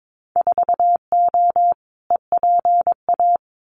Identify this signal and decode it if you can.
Morse code, heard as a keyed tone, about 22 wpm, 700 Hz. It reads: 4O EPA
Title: Winter Field Day Exchanges 22wpm